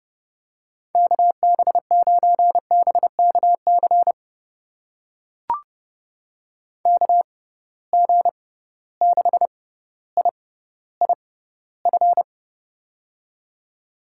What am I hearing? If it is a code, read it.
Morse code, heard as a keyed tone, about 30 wpm, spaced as for 15 wpm, 700 Hz. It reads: KB9BKC KG6IIF
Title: USA Call Signs - 6 Characters - 6x Character Spacing - 30wpm